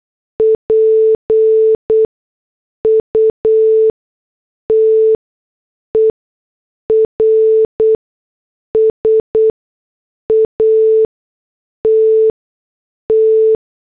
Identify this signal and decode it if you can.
Morse code, heard as a keyed tone, about 8 wpm, spaced as for 6 wpm, 440 Hz. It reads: PUTERS AT T
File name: morse / headlines